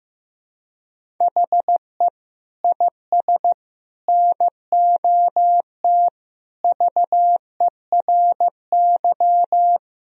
Morse code, heard as a keyed tone, about 15 wpm, 700 Hz. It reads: HE IS NOT VERY
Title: Sentences from Top 1000 Words - Review Part 1 - 15wpm